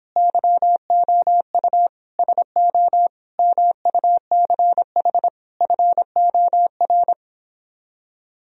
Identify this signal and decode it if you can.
Morse code, heard as a keyed tone, about 26 wpm, 700 Hz. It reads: YOU SO MUCH FOR